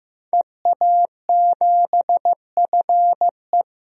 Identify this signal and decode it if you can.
Morse code, heard as a keyed tone, about 15 wpm, 700 Hz. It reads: EA7FE
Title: DXCC-100 Call Signs - 5 Characters - 15wpm